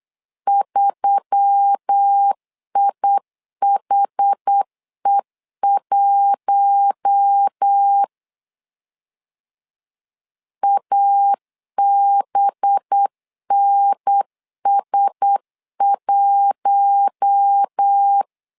Morse code, heard as a keyed tone, about 8 wpm, 800 Hz. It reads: 3IHE1 ABNS1
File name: LEZIONI / 5 LBVC / 5 LB 8wpm 2816